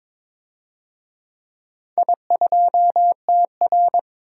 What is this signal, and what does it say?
Morse code, heard as a keyed tone, about 23 wpm, 700 Hz. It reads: I2TR
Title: DXCC-100 Call Signs - 4 Characters - 22wpm